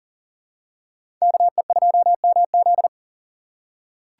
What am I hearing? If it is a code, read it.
Morse code, heard as a keyed tone, about 40 wpm, 700 Hz. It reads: KE2MZ
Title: USA Call Signs - 5 Characters - 2x Character Spacing - 40wpm